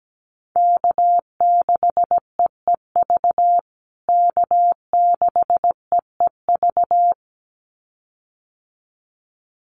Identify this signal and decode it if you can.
Morse code, heard as a keyed tone, about 17 wpm, 700 Hz. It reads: K6EEV K6EEV
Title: USA Call Signs Repeated 2x 17wpm